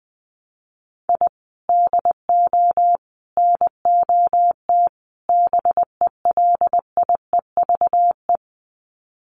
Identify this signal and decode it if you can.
Morse code, heard as a keyed tone, about 19 wpm, 700 Hz. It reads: I DO NOT BELIEVE
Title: Sets of 4 Words 20wpm